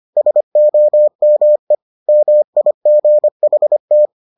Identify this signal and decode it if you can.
Morse code, heard as a keyed tone, about 25 wpm, 600 Hz. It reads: SOME MIGHT